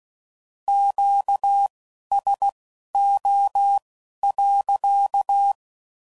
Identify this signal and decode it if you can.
Morse code, heard as a keyed tone, about 16 wpm, spaced as for 11 wpm, 800 Hz. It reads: QSO.
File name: CW Code Groups @15WPM / Q Calls / Q Calls sppc